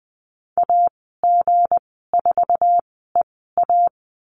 Morse code, heard as a keyed tone, about 20 wpm, spaced as for 15 wpm, 700 Hz. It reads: AG4EA
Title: 161 Lesson 26g - Letter g - Call Signs 20wpm